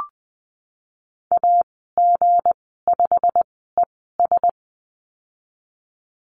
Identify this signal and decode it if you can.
Morse code, heard as a keyed tone, about 20 wpm, spaced as for 15 wpm, 700 Hz. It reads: AG5ES
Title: 161 Lesson 26g - Letter g - Call Signs 20wpm